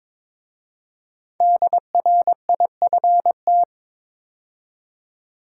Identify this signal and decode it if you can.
Morse code, heard as a keyed tone, about 23 wpm, 700 Hz. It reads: DRIFT
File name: 5 Letter Words - First 3 Letters Given - 22wpm